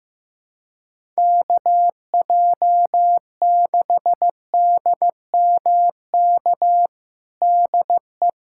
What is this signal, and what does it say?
Morse code, heard as a keyed tone, about 15 wpm, 700 Hz. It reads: KJ6DMK DE
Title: USA Callsign DE Callsign - 6 Characters - 15wpm